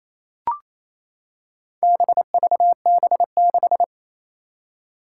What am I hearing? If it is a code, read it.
Morse code, heard as a keyed tone, about 28 wpm, 700 Hz. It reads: BVB6
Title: Four Challenging Letter-Number Combinations 28wpm